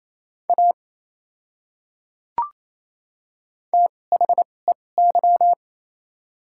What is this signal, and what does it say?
Morse code, heard as a keyed tone, about 28 wpm, spaced as for 20 wpm, 700 Hz. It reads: A THEY
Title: 2x Character Spacing - Top 100 Words - 28wpm